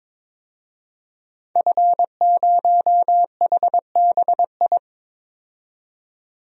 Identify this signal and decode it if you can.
Morse code, heard as a keyed tone, about 22 wpm, 700 Hz. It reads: F0HBI
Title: DXCC-100 Call Signs - 5 Characters - 22wpm